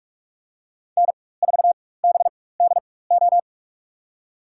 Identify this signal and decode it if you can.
Morse code, heard as a keyed tone, about 42 wpm, spaced as for 20 wpm, 700 Hz. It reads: N4BDO
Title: USA Call Signs - 5 Characters - 4x Character Spacing - 45wpm